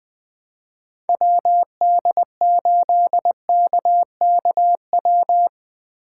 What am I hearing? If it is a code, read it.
Morse code, heard as a keyed tone, about 20 wpm, 700 Hz. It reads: WD8KKW